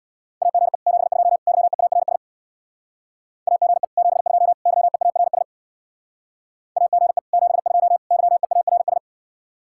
Morse code, heard as a keyed tone, about 64 wpm, 700 Hz. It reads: AGE 72 YEARS AGE 72 YEARS AGE 72 YEARS